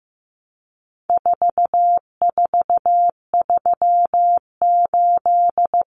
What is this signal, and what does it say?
Morse code, heard as a keyed tone, about 15 wpm, 700 Hz. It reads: 4438
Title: Numbers - 4-Digit 15wpm